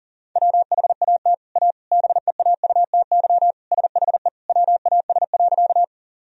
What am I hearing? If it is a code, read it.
Morse code, heard as a keyed tone, about 40 wpm, 700 Hz. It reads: WHAT A BEAUTY SHE WAS.
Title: Sentences from Top 1000 Words - Review Part 2 - 40wpm